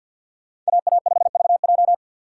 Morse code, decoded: AA5VJ

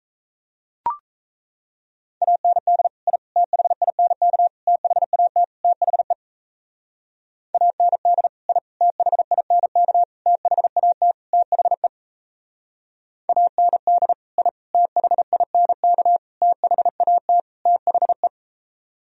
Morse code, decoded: AND I THINK THAT THE AND I THINK THAT THE AND I THINK THAT THE

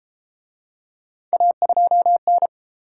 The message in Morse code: A2N

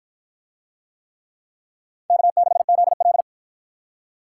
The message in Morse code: K6ZL

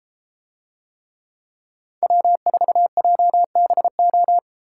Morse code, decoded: W4JBO

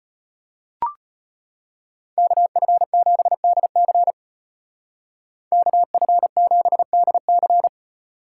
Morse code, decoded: KF7DC KF7DC

